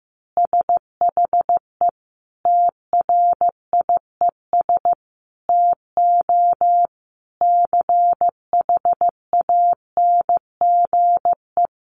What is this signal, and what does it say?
Morse code, heard as a keyed tone, about 15 wpm, 700 Hz. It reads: SHE TRIES TO CHANGE
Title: Easy Phrases 15wpm